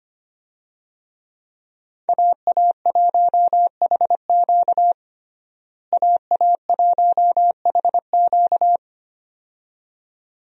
Morse code, heard as a keyed tone, about 25 wpm, 700 Hz. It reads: AA1HQ AA1HQ
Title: USA Call Signs - 5 Characters - Repeated 2x - 25wpm